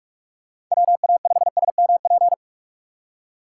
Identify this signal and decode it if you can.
Morse code, heard as a keyed tone, about 46 wpm, 700 Hz. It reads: WA5SMP
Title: USA Call Signs - 6 Characters - 4x Character Spacing - 45wpm